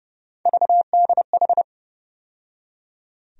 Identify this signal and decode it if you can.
Morse code, heard as a keyed tone, about 30 wpm, 700 Hz. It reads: VDH
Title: USA Call Sign Postfixes - 3 Letters 30wpm